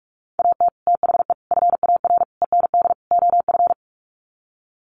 Morse code, heard as a keyed tone, about 44 wpm, 700 Hz. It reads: AT THE FAR END OF
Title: Sets of 5 Words 45wpm